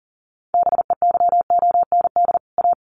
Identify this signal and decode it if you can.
Morse code, heard as a keyed tone, about 40 wpm, 700 Hz. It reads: BEYOND A
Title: Sentences from Top 1500 Words 40wpm